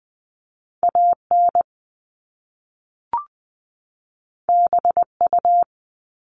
Morse code, heard as a keyed tone, about 20 wpm, 700 Hz. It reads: AN BU
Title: Top Two Letter Combinations in Words 20wpm